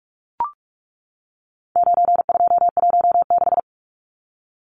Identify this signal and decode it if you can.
Morse code, heard as a keyed tone, about 44 wpm, 700 Hz. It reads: 9216
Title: Numbers - 4-Digit 45wpm